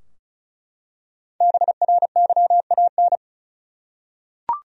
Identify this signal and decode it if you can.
Morse code, heard as a keyed tone, about 36 wpm, 700 Hz. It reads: BRYAN
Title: USA Common Names - Max 5 Letters - 35wpm